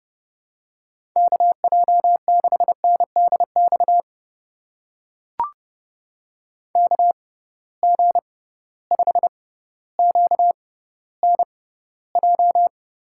Morse code, heard as a keyed tone, about 30 wpm, spaced as for 15 wpm, 700 Hz. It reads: KJ6NDX KG5QNJ
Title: USA Call Signs - 6 Characters - 6x Character Spacing - 30wpm